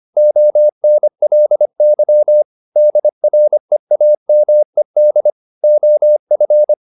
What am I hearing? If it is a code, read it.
Morse code, heard as a keyed tone, about 25 wpm, 600 Hz. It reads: ONLY DREAMED OF
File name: the old man and the sea part 1 25wpm